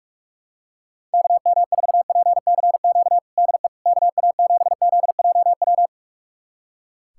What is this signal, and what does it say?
Morse code, heard as a keyed tone, about 41 wpm, 700 Hz. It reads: KM4PCQ DE KA7ZJW
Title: USA Callsign DE Callsign - 6 Characters - 45wpm